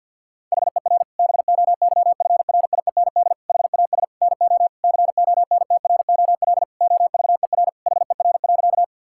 Morse code, heard as a keyed tone, about 50 wpm, 700 Hz. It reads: HER BOYFRIEND HAS NO CONTROL OVER HER.
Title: Sentences from Top 1000 Words - Review Part 5 - 50wpm